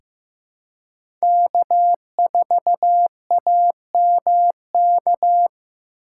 Morse code, heard as a keyed tone, about 15 wpm, 700 Hz. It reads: K4AMK
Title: USA Callsign DE Callsign - 5 Characters - 15wpm